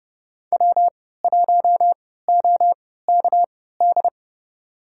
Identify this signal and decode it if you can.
Morse code, heard as a keyed tone, about 29 wpm, spaced as for 17 wpm, 700 Hz. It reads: W1OKD
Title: USA Call Signs - 5 Characters - 3x Character Spacing - 30wpm